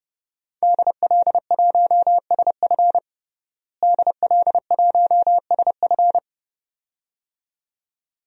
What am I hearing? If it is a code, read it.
Morse code, heard as a keyed tone, about 30 wpm, 700 Hz. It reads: DL1SF DL1SF
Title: DXCC-100 Call Signs - 5 Characters - Repeated 2x - 30wpm